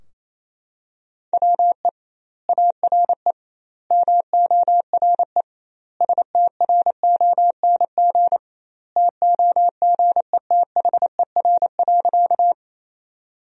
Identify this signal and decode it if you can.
Morse code, heard as a keyed tone, about 28 wpm, 700 Hz. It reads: WE ARE MORE STRONG TOGETHER.
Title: 2x Word-Spacing - Sentences from Top 500 Words - 28wpm